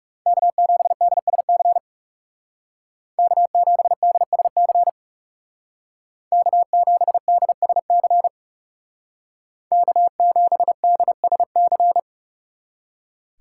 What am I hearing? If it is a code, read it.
Morse code, heard as a keyed tone, about 35 wpm, 700 Hz. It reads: K7DSC K7DSC K7DSC K7DSC
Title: Speed Racing - USA Call Signs 30wpm